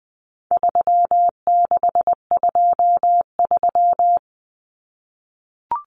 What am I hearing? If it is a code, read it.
Morse code, heard as a keyed tone, about 20 wpm, 700 Hz. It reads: 3623
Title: Numbers - 4-Digit 20wpm